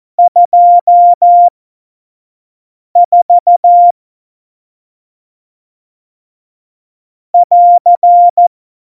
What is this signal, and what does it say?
Morse code, heard as a keyed tone, about 14 wpm, spaced as for 5 wpm, 700 Hz. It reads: 24 <AR>
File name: Morsekurs / Witze / Witz 13 March 2018, 25 F70